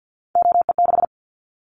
Morse code, heard as a keyed tone, about 50 wpm, 700 Hz. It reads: OE6
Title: Callsign Prefixes for Top 100 DXCC - 50wpm